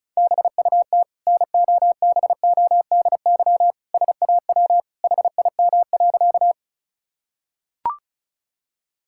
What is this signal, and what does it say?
Morse code, heard as a keyed tone, about 36 wpm, 700 Hz. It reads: BUT NOBODY SAW HIM.
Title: Sentences from Top 1600 Words 35wpm